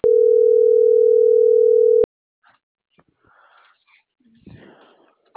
Ringback tone (North America), 440 + 480 Hz.